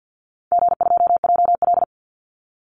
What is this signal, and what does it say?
Morse code, heard as a keyed tone, about 48 wpm, 700 Hz. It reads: G2JL
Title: Call Signs - 50wpm